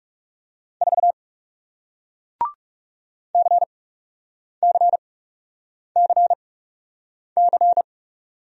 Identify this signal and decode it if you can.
Morse code, heard as a keyed tone, about 40 wpm, spaced as for 8 wpm, 700 Hz. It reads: 4 CCCC